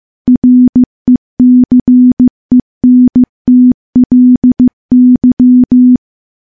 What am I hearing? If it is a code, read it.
Morse code, heard as a keyed tone, about 16 wpm, 250 Hz. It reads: RECENTLY
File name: morse / morse-training-15WPM